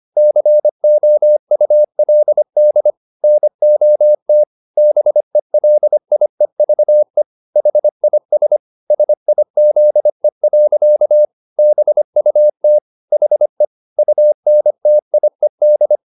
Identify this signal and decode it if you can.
Morse code, heard as a keyed tone, about 25 wpm, 600 Hz. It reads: COULD NOT BELIEVE HIS SIZE. BUT HE UNTIED